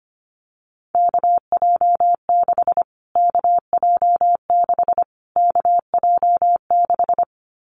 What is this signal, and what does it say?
Morse code, heard as a keyed tone, about 25 wpm, 700 Hz. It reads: KJ6 KJ6 KJ6
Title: Callsign Prefixes for Top 25 DXCC - Repeated 3x - 25wpm